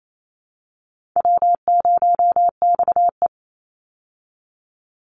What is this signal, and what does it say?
Morse code, heard as a keyed tone, about 28 wpm, 700 Hz. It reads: W0XE
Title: Easy Call Signs Repeated 3x - 28wpm